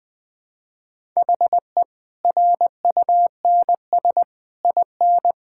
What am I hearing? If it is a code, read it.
Morse code, heard as a keyed tone, about 20 wpm, 700 Hz. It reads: HE RUNS IN
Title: Easy Sentences with Hints - 20wpm